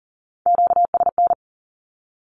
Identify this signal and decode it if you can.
Morse code, heard as a keyed tone, about 42 wpm, 700 Hz. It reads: QSN
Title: Q-Codes with Spoken Meaning - 40wpm